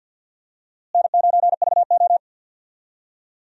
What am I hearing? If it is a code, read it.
Morse code, heard as a keyed tone, about 52 wpm, 700 Hz. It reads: N9VO
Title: USA Call Signs - 4 Characters - Repeated 2x - 50wpm